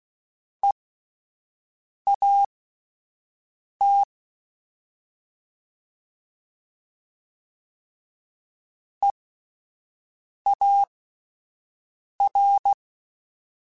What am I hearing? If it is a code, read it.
Morse code, heard as a keyed tone, about 16 wpm, spaced as for 5 wpm, 800 Hz. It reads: EAT EAR